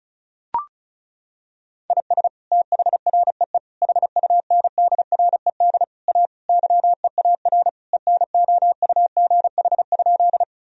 Morse code, heard as a keyed tone, about 34 wpm, 700 Hz. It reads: IS THREE HUNDRED A YEAR ENOUGH?
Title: Sentences from Top 1000 Words - Review Part 2 - 35wpm